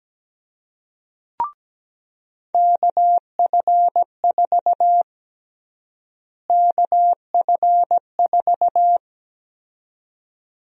Morse code, heard as a keyed tone, about 17 wpm, 700 Hz. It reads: KF4 KF4
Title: USA Call Sign Prefixes - Repeated 2x - 17wpm